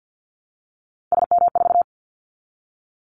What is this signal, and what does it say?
Morse code, heard as a keyed tone, about 50 wpm, 700 Hz. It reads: SM4
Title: Callsign Prefixes for Top 25 DXCC - Repeated 3x - 50wpm